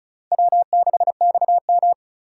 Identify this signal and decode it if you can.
Morse code, heard as a keyed tone, about 46 wpm, 700 Hz. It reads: W6XM